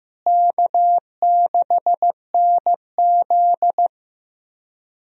Morse code, heard as a keyed tone, about 15 wpm, 700 Hz. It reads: K6NZ
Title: USA Call Signs - 4 Characters - Repeated 3x - 15wpm